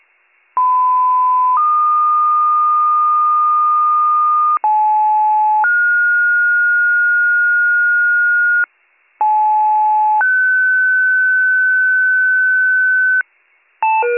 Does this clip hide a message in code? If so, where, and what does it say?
Yes: two-tone page from 0.6 to 13.2 s, heard as a one-second tone, then a three-second tone, about 1010 then 1220 Hz, about 860 then 1430 Hz, about 860 then 1500 Hz, with noise.